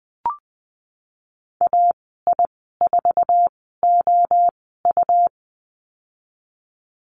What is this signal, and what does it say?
Morse code, heard as a keyed tone, about 20 wpm, spaced as for 15 wpm, 700 Hz. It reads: AI4OU
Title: 128 Lesson 21g - Number 6 - Call Signs 20wpm